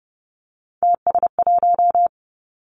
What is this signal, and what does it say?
Morse code, heard as a keyed tone, about 30 wpm, 700 Hz. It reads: TS1
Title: Three Challenging Letter-Number Combinations 30wpm